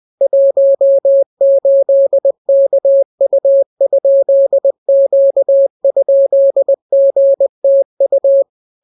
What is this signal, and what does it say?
Morse code, heard as a keyed tone, about 20 wpm, 550 Hz. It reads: 18KU?Q?GTU